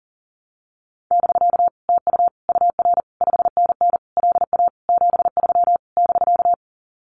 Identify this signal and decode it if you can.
Morse code, heard as a keyed tone, about 40 wpm, 700 Hz. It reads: <BK> TU UR 5NN LA 73 <BK>
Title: Common QSO Phrases Encore 40wpm